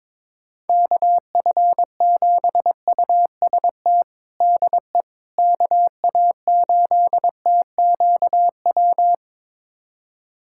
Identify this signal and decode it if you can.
Morse code, heard as a keyed tone, about 22 wpm, 700 Hz. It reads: KF7UST DE KA8TQW
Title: USA Callsign DE Callsign 22wpm